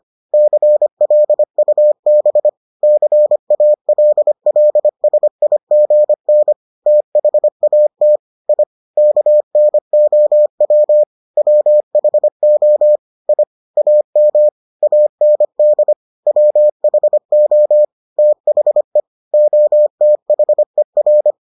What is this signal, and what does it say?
Morse code, heard as a keyed tone, about 25 wpm, 600 Hz. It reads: CLUB CALLSIGN THAT I KNOW WHO I AM AND WHO THE OTHER